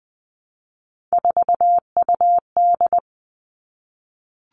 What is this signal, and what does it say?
Morse code, heard as a keyed tone, about 20 wpm, 700 Hz. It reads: 4UD
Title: All Cut Numbers - 3-Digit 20wpm